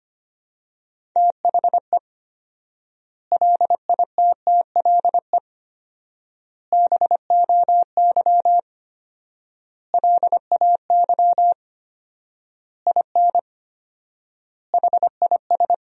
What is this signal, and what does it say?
Morse code, heard as a keyed tone, about 25 wpm, 700 Hz. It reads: THE LITTLE BOY LAY IN HIS
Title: 4x Word-Spacing - Sentences from Top 500 Words - 25wpm